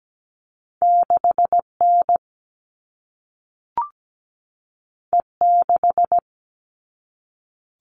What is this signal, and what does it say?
Morse code, heard as a keyed tone, about 17 wpm, 700 Hz. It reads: 6N E6
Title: All Cut Numbers - 2-Digit 17wpm